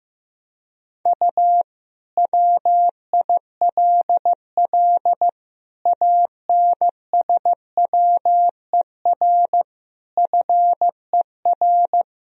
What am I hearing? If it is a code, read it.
Morse code, heard as a keyed tone, about 16 wpm, 700 Hz. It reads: U WILL ANSWER FER
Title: Sentences from Top 300 Words and Abbreviations 15wpm